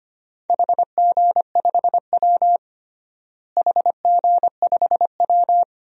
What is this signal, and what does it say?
Morse code, heard as a keyed tone, about 25 wpm, 700 Hz. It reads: HG5W HG5W
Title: DXCC-100 Call Signs - 4 Characters - Repeated 2x - 25wpm